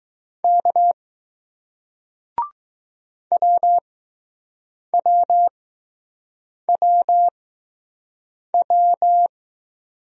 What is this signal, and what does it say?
Morse code, heard as a keyed tone, about 20 wpm, spaced as for 6 wpm, 700 Hz. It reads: K WWWW